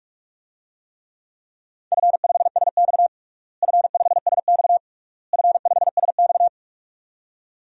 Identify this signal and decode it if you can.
Morse code, heard as a keyed tone, about 46 wpm, 700 Hz. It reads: F5SX F5SX F5SX
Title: DXCC-100 Call Signs - 4 Characters - Repeated 3x - 45wpm